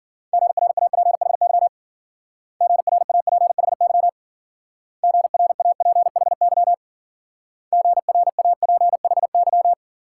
Morse code, heard as a keyed tone, about 50 wpm, 700 Hz. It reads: GRAPHY GRAPHY GRAPHY GRAPHY
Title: Speed-Racing Multi-letter Phonemes - 40wpm